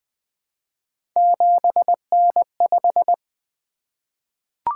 Morse code, heard as a keyed tone, about 20 wpm, 700 Hz. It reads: 7N5
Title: Three Letter-Number Combinations - 20wpm